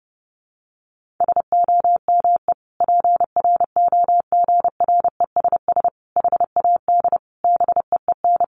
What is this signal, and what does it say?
Morse code, heard as a keyed tone, about 30 wpm, 700 Hz. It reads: SOME PROGRESS HAD BEEN
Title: Sentences from Top 1400 Words 30wpm